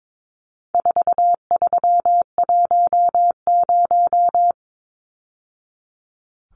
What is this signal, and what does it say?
Morse code, heard as a keyed tone, about 22 wpm, 700 Hz. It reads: 4310
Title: Numbers - 4-Digit 22wpm